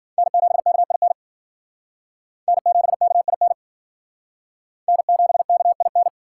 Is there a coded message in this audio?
Yes: Morse code from 0.2 to 6.1 s, heard as a keyed tone, about 56 wpm, 700 Hz. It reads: N7KIN N7KIN N7KIN